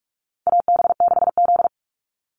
Morse code, heard as a keyed tone, about 44 wpm, 700 Hz. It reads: AB6Z
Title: Call Signs Repeated 2x - 45wpm